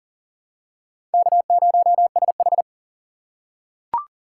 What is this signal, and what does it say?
Morse code, heard as a keyed tone, about 40 wpm, 700 Hz. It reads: K0SH
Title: USA Call Signs - 4 Characters - 2x Character Spacing - 40wpm